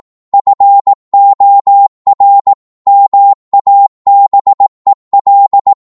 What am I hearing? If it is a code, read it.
Morse code, heard as a keyed tone, about 18 wpm, 800 Hz. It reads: FOR MABEL